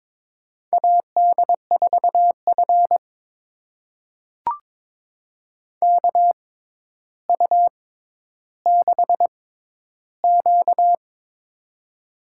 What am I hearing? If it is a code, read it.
Morse code, heard as a keyed tone, about 22 wpm, spaced as for 11 wpm, 700 Hz. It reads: AD4F KU6Q
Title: USA Call Signs - 4 Characters - 6x Character Spacing - 22wpm